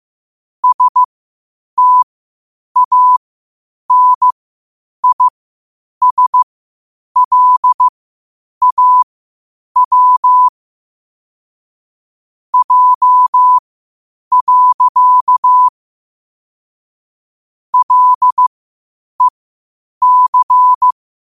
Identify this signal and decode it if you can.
Morse code, heard as a keyed tone, about 15 wpm, spaced as for 8 wpm, 1000 Hz. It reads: STANISLAW J. LEC
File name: qotdfiles / 8wpm